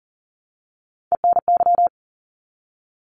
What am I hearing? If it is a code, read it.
Morse code, heard as a keyed tone, about 40 wpm, 700 Hz. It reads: ENY
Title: ARRL Sections 40wpm